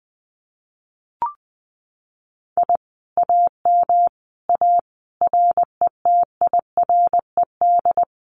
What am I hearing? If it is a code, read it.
Morse code, heard as a keyed tone, about 20 wpm, 700 Hz. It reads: I AM A RETIRED
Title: Common QSO Phrases Encore 20wpm